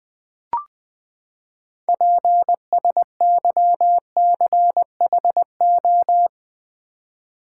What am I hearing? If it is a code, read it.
Morse code, heard as a keyed tone, about 20 wpm, 700 Hz. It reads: PSYCHO
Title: Build A Word - 6 Letters At A Time - 12-Letter Words - 20wpm